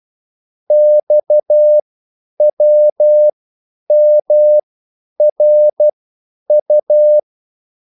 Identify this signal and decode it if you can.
Morse code, heard as a keyed tone, about 12 wpm, spaced as for 9 wpm, 600 Hz. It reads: XWMRU